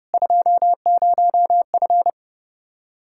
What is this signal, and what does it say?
Morse code, heard as a keyed tone, about 30 wpm, 700 Hz. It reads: 20F